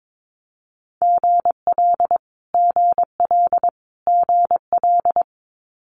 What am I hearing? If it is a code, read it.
Morse code, heard as a keyed tone, about 22 wpm, 700 Hz. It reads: GL GL GL
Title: Multi-letter Phonemes Repeated 3x 22wpm